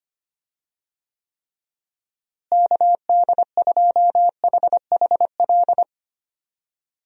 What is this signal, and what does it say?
Morse code, heard as a keyed tone, about 25 wpm, 700 Hz. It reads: KD2HHL
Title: USA Call Signs - 6 Characters - Repeated 3x - 25wpm